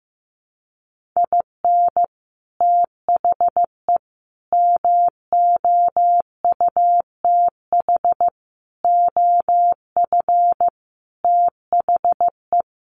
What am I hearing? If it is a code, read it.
Morse code, heard as a keyed tone, about 15 wpm, 700 Hz. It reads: IN THE MOUTH OF THE
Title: Sentences from Top 1500 Words 15wpm